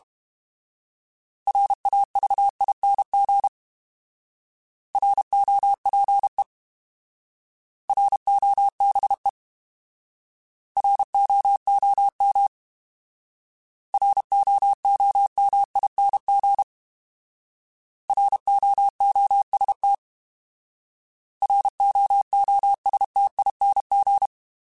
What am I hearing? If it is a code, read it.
Morse code, heard as a keyed tone, about 32 wpm, 800 Hz. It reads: RAVING ROPE ROBE ROOM ROOMING ROOST ROOSTING
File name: Alphabetical / Alphabetical mp3s / Rs @30